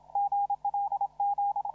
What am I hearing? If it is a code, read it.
Morse code, heard as a keyed tone, about 33 wpm, 800 Hz, with radio noise. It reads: GLG
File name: CW / RHOB CW 2017 HA6P 0001